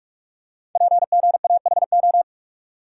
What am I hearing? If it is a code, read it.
Morse code, heard as a keyed tone, about 44 wpm, 690 Hz. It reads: PGAHO
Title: Code Groups - 5 Letters - 45wpm